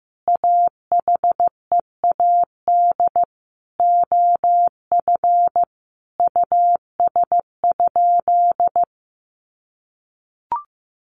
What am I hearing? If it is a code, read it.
Morse code, heard as a keyed tone, about 15 wpm, 700 Hz. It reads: AHEAD OF US?
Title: Sentences from Top 1100 Words 15wpm